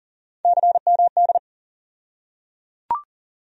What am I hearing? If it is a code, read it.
Morse code, heard as a keyed tone, about 38 wpm, 700 Hz. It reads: CMD
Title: USA Call Sign Postfixes - 3 Letters 40wpm